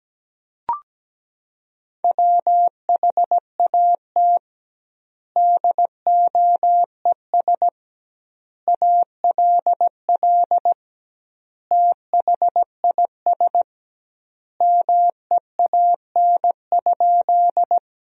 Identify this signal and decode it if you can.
Morse code, heard as a keyed tone, about 17 wpm, 700 Hz. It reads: WHAT DOES ALL THIS MEAN?
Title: 2x Word-Spacing - Sentences from Top 200 Words - 17wpm